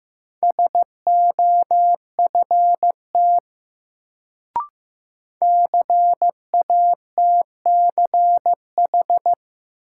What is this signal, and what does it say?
Morse code, heard as a keyed tone, about 15 wpm, 700 Hz. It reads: SOFT CATCH
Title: Sentences from Top 1500 Words - New Word Drill - 15wpm